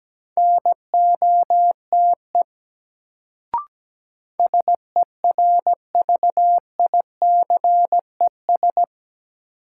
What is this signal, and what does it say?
Morse code, heard as a keyed tone, about 17 wpm, 700 Hz. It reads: NOTE SERVICES
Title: Sets of 5 Words - New Word Drill - 17wpm